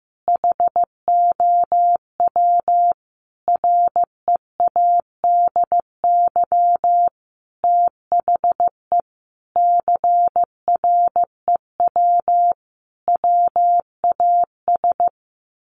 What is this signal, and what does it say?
Morse code, heard as a keyed tone, about 15 wpm, 700 Hz. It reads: HOW READY THE CREW WAS
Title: Sentences from Top 1500 Words 15wpm